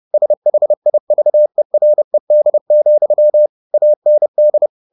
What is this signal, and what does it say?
Morse code, heard as a keyed tone, about 30 wpm, 600 Hz. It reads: SHIVERED, AND